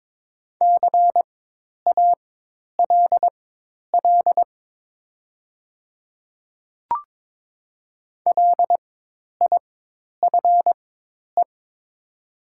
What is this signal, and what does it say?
Morse code, heard as a keyed tone, about 22 wpm, spaced as for 10 wpm, 700 Hz. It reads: CALL LIFE
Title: 4x Character Spacing - Top 300 Words - 22wpm